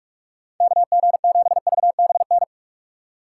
Morse code, heard as a keyed tone, about 42 wpm, 700 Hz. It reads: KG7VBN